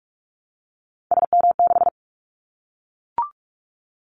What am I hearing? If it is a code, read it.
Morse code, heard as a keyed tone, about 44 wpm, 700 Hz. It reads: SM6
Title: Callsign Prefixes for Top 25 DXCC - Repeated 3x - 45wpm